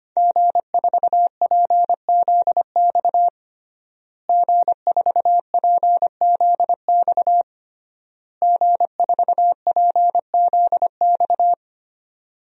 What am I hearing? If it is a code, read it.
Morse code, heard as a keyed tone, about 25 wpm, 700 Hz. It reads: G4PZX G4PZX G4PZX
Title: DXCC-100 Call Signs - 5 Characters - Repeated 3x - 25wpm